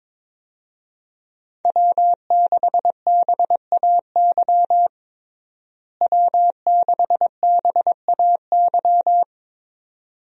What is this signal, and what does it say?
Morse code, heard as a keyed tone, about 22 wpm, 700 Hz. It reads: W6BAY W6BAY
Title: USA Call Signs - 5 Characters - Repeated 2x - 22wpm